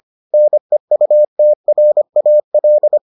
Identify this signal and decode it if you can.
Morse code, heard as a keyed tone, about 25 wpm, 600 Hz. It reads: NEUTRAL